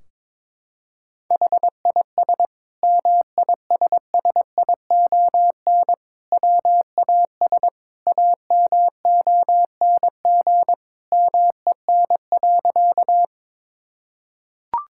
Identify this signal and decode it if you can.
Morse code, heard as a keyed tone, about 22 wpm, 700 Hz. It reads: HIS MISSION WAS AMONG MEN.
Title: Sentences from Top 1600 Words 22wpm